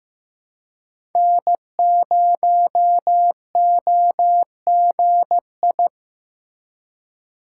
Morse code, heard as a keyed tone, about 15 wpm, 700 Hz. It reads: N0OGI
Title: USA Call Signs - 5 Characters - 3x Character Spacing - 15wpm